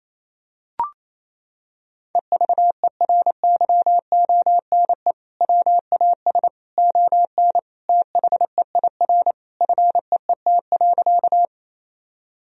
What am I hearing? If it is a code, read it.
Morse code, heard as a keyed tone, about 28 wpm, 700 Hz. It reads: EVERYONE WAS ON THEIR FEET.
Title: Sentences from Top 1000 Words - Review Part 1 - 28wpm